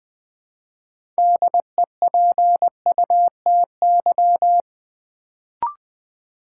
Morse code, heard as a keyed tone, about 20 wpm, 700 Hz. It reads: DEPUTY
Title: Build A Word - 3 Letters At A Time - 6-Letter Words - 20wpm